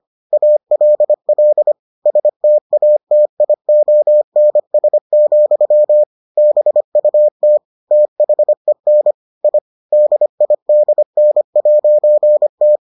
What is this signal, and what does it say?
Morse code, heard as a keyed tone, about 25 wpm, 600 Hz. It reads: ALL STATIONS, BUT THEN I DIDN'T